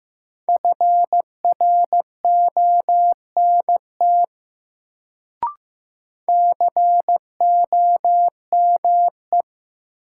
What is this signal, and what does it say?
Morse code, heard as a keyed tone, about 15 wpm, 700 Hz. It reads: FRONT COME
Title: Sets of 2 Words - New Word Drill - 15wpm